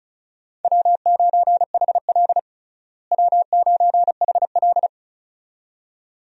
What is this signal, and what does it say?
Morse code, heard as a keyed tone, about 34 wpm, 700 Hz. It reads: W9HL W9HL
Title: USA Call Signs - 4 Characters - Repeated 2x - 35wpm